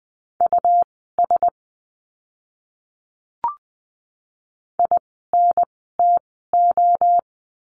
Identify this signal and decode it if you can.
Morse code, heard as a keyed tone, about 20 wpm, spaced as for 14 wpm, 700 Hz. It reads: US INTO
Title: 230 Lesson 37f - Letter x - Words 20wpm